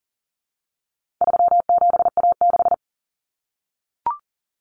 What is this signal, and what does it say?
Morse code, heard as a keyed tone, about 40 wpm, 700 Hz. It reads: 37A6
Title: Common Cut Numbers - 4-Digit 40wpm